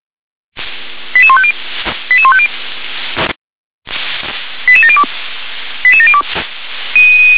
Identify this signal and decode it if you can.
Five-tone selective call (ZVEI), heard as quick runs of five tones, tones of about 70 ms. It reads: 88150 88150 88702 88702